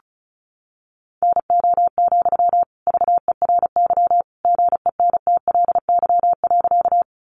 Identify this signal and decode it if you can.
Morse code, heard as a keyed tone, about 34 wpm, 700 Hz. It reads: NO, VERY GENTLY.
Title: Sentences from Top 1500 Words 35wpm